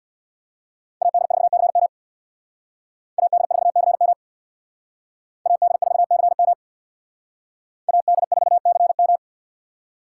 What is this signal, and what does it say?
Morse code, heard as a keyed tone, about 62 wpm, 700 Hz. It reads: AD4CM AD4CM AD4CM AD4CM
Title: Speed Racing - USA 2x2 Call Signs 50wpm